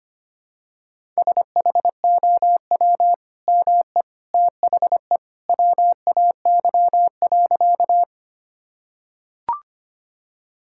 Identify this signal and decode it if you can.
Morse code, heard as a keyed tone, about 25 wpm, 700 Hz. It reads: SHOW ME THE WAY.
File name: practice / 25wpm / Sentences with 2-Word Phrases - 25wpm